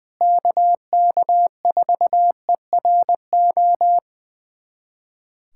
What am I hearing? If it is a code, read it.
Morse code, heard as a keyed tone, about 21 wpm, 700 Hz. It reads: KK4ERO
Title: USA Callsign DE Callsign 20wpm